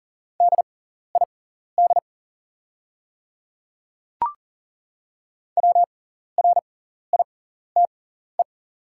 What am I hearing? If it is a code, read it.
Morse code, heard as a keyed tone, about 40 wpm, spaced as for 14 wpm, 690 Hz. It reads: DID WRITE
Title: 6x Character Spacing - Top 200 Words - 40wpm